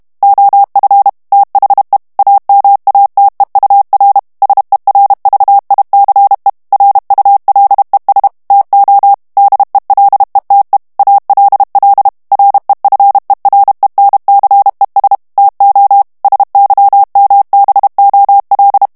Morse code, heard as a keyed tone, about 32 wpm, 800 Hz. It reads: OF THE AMATEUR SERVICE RULES TO DELETE ALL REFERENCES TO SYMBOL
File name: morse mp3 / ANGLAIS texte clair 3 vitesse 30 mots minute